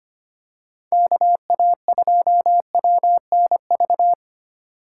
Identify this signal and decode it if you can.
Morse code, heard as a keyed tone, about 25 wpm, 700 Hz. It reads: KA2WNV
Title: USA Call Signs - 6 Characters - 3x Character Spacing - 25wpm